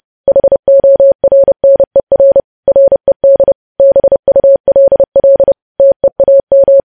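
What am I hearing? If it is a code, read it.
Morse code, heard as a keyed tone, about 30 wpm, 550 Hz. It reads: HORNER RED BULL TEAM
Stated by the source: Morse code sending of news headlines at 30 wpm.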